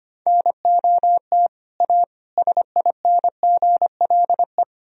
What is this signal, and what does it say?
Morse code, heard as a keyed tone, about 25 wpm, 700 Hz. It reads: NOT A SINGLE
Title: Sentences from Top 1000 Words - Review Part 4 - 25wpm